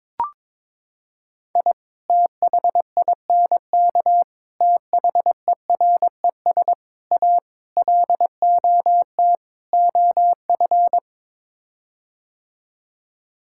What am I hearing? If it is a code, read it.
Morse code, heard as a keyed tone, about 22 wpm, 700 Hz. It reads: I THINK THERES A LOT OF
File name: practice / 22wpm / Sets of 6 Words - 22wpm